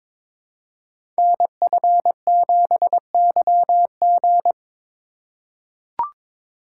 Morse code, heard as a keyed tone, about 22 wpm, 700 Hz. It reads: NF7YG
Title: Code Groups - 5 Letter:Numbers - 22wpm